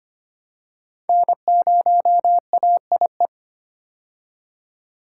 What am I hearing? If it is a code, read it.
Morse code, heard as a keyed tone, about 25 wpm, 700 Hz. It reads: N0AIE
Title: K1USN SST QSO Practice - 25wpm